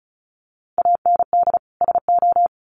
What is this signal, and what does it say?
Morse code, heard as a keyed tone, about 34 wpm, 700 Hz. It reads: AND SO
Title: Sets of 2 Words 35wpm